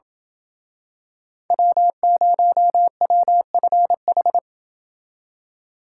Morse code, heard as a keyed tone, about 28 wpm, 700 Hz. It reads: W0WFH